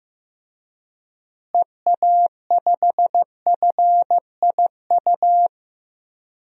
Morse code, heard as a keyed tone, about 15 wpm, 700 Hz. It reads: EA5FIU
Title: DXCC-100 Call Signs - 6 Characters - Repeated 2x - 15wpm